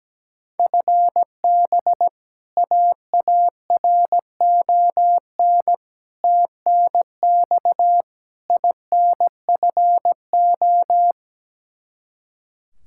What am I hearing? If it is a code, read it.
Morse code, heard as a keyed tone, about 17 wpm, 700 Hz. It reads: FB AARON TNX INFO
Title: K1USN SST QSO Practice 17wpm